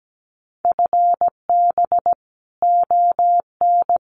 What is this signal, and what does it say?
Morse code, heard as a keyed tone, about 17 wpm, 700 Hz. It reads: FB ON
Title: Common QSO Phrases Encore 17wpm